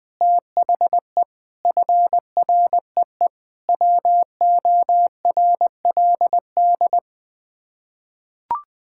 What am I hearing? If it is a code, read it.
Morse code, heard as a keyed tone, about 20 wpm, 700 Hz. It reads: THE FREE WORLD
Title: Sets of 6 Words - 20wpm